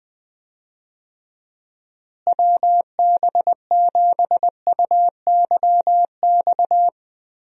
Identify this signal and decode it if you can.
Morse code, heard as a keyed tone, about 21 wpm, 700 Hz. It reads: WB7UYX
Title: USA Call Signs - 6 Characters - Repeated 3x - 20wpm